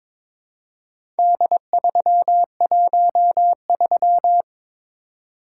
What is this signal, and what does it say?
Morse code, heard as a keyed tone, about 22 wpm, 690 Hz. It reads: D313
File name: practice / 22wpm / Four Letter-Number Combinations - 22wpm